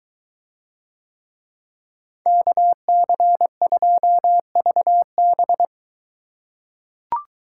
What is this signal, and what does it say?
Morse code, heard as a keyed tone, about 21 wpm, 700 Hz. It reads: KC2VB